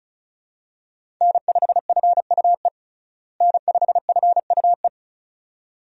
Morse code, heard as a keyed tone, about 36 wpm, 700 Hz. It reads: N5FUE N5FUE
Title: USA Call Signs - 5 Characters - Repeated 2x - 35wpm